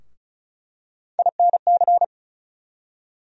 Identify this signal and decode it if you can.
Morse code, heard as a keyed tone, about 36 wpm, 700 Hz. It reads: INC
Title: USA Call Sign Postfixes - 3 Letters 35wpm